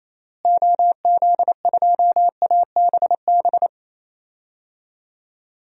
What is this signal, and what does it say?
Morse code, heard as a keyed tone, about 28 wpm, 700 Hz. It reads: OZ2ABB
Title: DXCC-100 Call Signs - 6 Characters - Repeated 3x - 28wpm